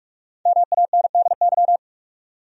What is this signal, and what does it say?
Morse code, heard as a keyed tone, about 46 wpm, 700 Hz. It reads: MANDY
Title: USA Common Names - Max 5 Letters - 45wpm